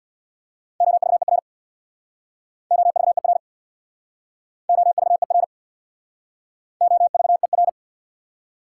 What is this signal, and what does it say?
Morse code, heard as a keyed tone, about 61 wpm, 690 Hz. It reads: OVER OVER OVER OVER
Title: Speed-Racing Prefixes and Suffixes 50wpm